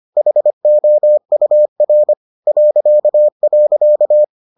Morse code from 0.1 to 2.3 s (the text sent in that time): HOUR